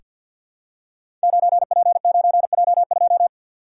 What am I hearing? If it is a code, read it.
Morse code, heard as a keyed tone, about 50 wpm, 700 Hz. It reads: 9P9J2
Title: Code Groups - 5 Letter:Numbers - 50wpm